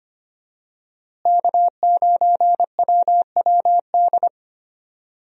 Morse code, heard as a keyed tone, about 25 wpm, 700 Hz. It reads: K9WWD
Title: USA Call Signs - 5 Characters - Repeated 3x - 25wpm